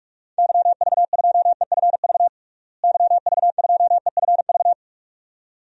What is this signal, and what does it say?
Morse code, heard as a keyed tone, about 41 wpm, 700 Hz. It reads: YV2EFV YV2EFV
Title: DXCC-100 Call Signs - 6 Characters - Repeated 2x - 45wpm